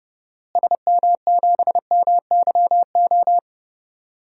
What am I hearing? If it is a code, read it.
Morse code, heard as a keyed tone, about 30 wpm, 700 Hz. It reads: SM7MYO
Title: DXCC-100 Call Signs - 6 Characters - Repeated 2x - 30wpm